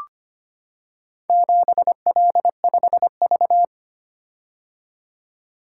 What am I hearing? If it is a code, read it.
Morse code, heard as a keyed tone, about 25 wpm, 700 Hz. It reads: 7L5V
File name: practice / 25wpm / Four Letter-Number Combinations - 25wpm